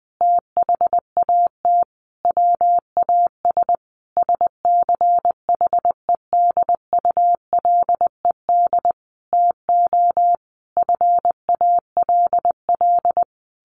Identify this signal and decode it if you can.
Morse code, heard as a keyed tone, about 20 wpm, 700 Hz. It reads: THAT WAS SCHEDULED TO FALL
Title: A Sound Of Thunder 20wpm